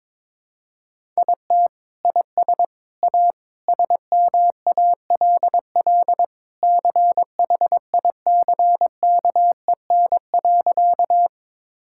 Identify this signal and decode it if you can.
Morse code, heard as a keyed tone, about 21 wpm, 700 Hz. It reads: IT IS A SMALL CHICKEN.
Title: Sentences from Top 1000 Words - Review Part 5 - 22wpm